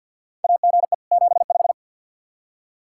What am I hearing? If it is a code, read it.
Morse code, heard as a keyed tone, about 53 wpm, 700 Hz. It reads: AGE 75
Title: Practice QSO - Exchange 3 - More Info - 50wpm